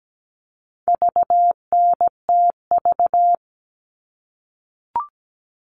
Morse code, heard as a keyed tone, about 17 wpm, 700 Hz. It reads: VNTV
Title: All Cut Numbers - 4-Digit 17wpm